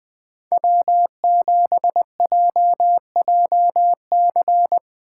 Morse code, heard as a keyed tone, about 20 wpm, 700 Hz. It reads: W7JJC
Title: Speed Racing - USA 1x3 Call Signs 20wpm